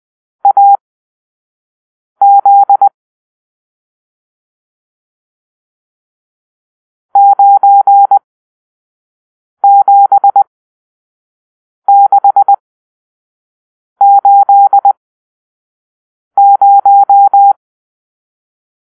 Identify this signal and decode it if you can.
Morse code, heard as a keyed tone, about 20 wpm, spaced as for 5 wpm, 800 Hz. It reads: AZ 97680
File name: arrl / Grp-7 Drill-3